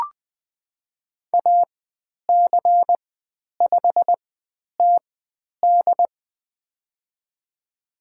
Morse code, heard as a keyed tone, about 20 wpm, spaced as for 10 wpm, 700 Hz. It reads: AC5TD
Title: 095 Lesson 16g - Letter c - Call Signs 20wpm